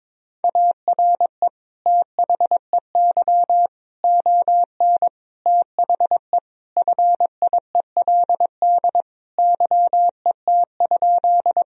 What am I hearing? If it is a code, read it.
Morse code, heard as a keyed tone, about 22 wpm, 700 Hz. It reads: ARE THEY ON THE FIELD YET?
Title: Sentences from Top 1000 Words - Review Part 2 - 22wpm